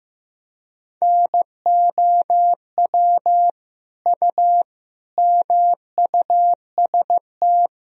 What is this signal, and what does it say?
Morse code, heard as a keyed tone, about 15 wpm, 700 Hz. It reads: NOW U MUST
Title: Sentences from Top 300 Words and Abbreviations 15wpm